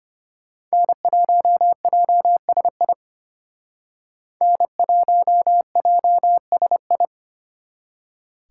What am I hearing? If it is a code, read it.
Morse code, heard as a keyed tone, about 32 wpm, 700 Hz. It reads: N1JSI N1JSI